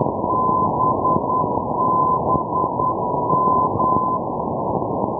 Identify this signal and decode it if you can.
Morse code, heard as a keyed tone, about 10 wpm, 1020 Hz, with noise. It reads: BBM
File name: europe / pol / pol-480 ndb zielona gora - babimost bbm-390-rb-d-bastei-2012